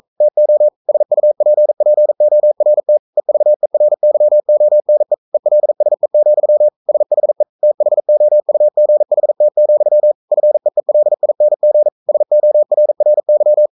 Morse code, heard as a keyed tone, about 42 wpm, 600 Hz. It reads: TO SUPPORT EVERYONE ELSE, SHE THOUGHT, FEELING SORRY
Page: Romanen Never af Ken Follet på engelsk.